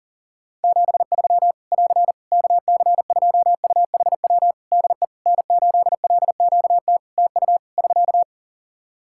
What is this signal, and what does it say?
Morse code, heard as a keyed tone, about 40 wpm, 700 Hz. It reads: 73 <AR> KC2UHW DE N8LQT TU <SK>
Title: Practice QSO - Exchange 4 - Ending - 40wpm